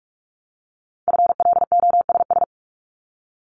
Morse code, heard as a keyed tone, about 46 wpm, 700 Hz. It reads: FLOSS
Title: Multi-letter Phonemes 45wpm